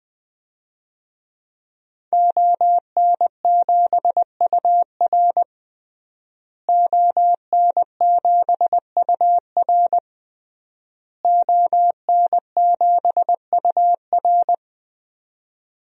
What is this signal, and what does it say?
Morse code, heard as a keyed tone, about 20 wpm, 700 Hz. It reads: ON7UR ON7UR ON7UR
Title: DXCC-100 Call Signs - 5 Characters - Repeated 3x - 20wpm